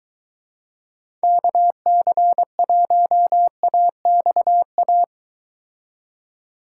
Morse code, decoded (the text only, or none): KC1AXA